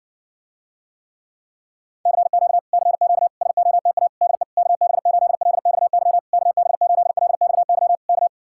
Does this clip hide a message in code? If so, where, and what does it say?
Yes: Morse code from 2.1 to 8.3 s, heard as a keyed tone, about 60 wpm, 700 Hz. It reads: CQ CQ SOTA DE KB8LXY KB8LXY K